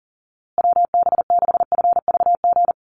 Morse code, heard as a keyed tone, about 40 wpm, 700 Hz. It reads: WB6FVG
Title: USA Call Signs Repeated 3x 40wpm